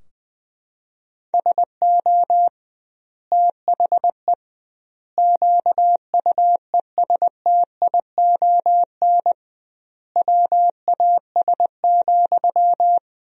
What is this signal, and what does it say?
Morse code, heard as a keyed tone, about 20 wpm, 700 Hz. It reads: SO THE QUESTION WAS,
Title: 2x Word-Spacing - Sentences from Top 400 Words - 20wpm